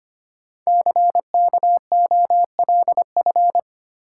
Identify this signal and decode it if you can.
Morse code, heard as a keyed tone, about 25 wpm, 700 Hz. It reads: CKOLF
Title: Code Groups - 5 Letters - 25wpm